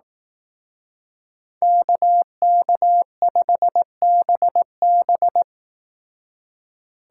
Morse code, heard as a keyed tone, about 18 wpm, 700 Hz. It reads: KK5BB
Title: Speed Racing - USA 2x2 Call Signs 15wpm